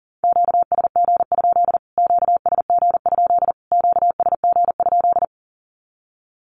Morse code, heard as a keyed tone, about 37 wpm, 700 Hz. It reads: QSG? QSG? QSG?
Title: Q-Codes Repeated 3x 40wpm